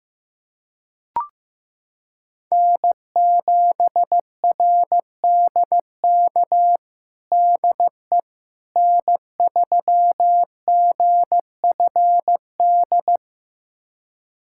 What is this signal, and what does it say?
Morse code, heard as a keyed tone, about 15 wpm, 700 Hz. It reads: N7RDK DE N3GFD
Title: USA Callsign DE Callsign - 5 Characters - 15wpm